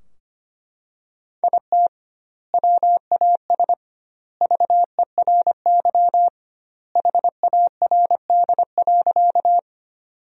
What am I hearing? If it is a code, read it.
Morse code, heard as a keyed tone, about 25 wpm, 700 Hz. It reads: IT WAS VERY HARD.
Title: 2x Word-Spacing - Sentences from Top 300 Words - 25wpm